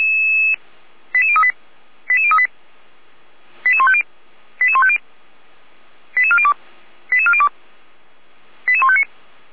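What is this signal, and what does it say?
Five-tone selective call (ZVEI), heard as quick runs of five tones, tones of about 70 ms. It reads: 88038 88038 88150 88150 88402 88402 88159
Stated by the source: ZVEI five-tone selective call received on SDR.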